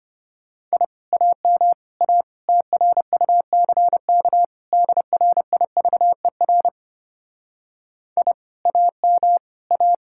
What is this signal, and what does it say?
Morse code, heard as a keyed tone, about 31 wpm, 700 Hz. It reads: I AM A TRUCK DRIVER I AM A
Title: Speed-Racing Common QSO Phrases - 25wpm